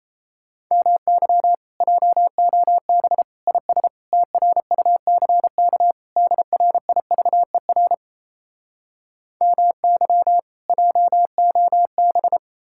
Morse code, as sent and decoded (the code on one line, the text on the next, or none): -- -.-- / .--- --- -... / .. ... / - .-. ..- -.-. -.- / -.. .-. .. ...- . .-. / -- -.-- / .--- --- -...
MY JOB IS TRUCK DRIVER MY JOB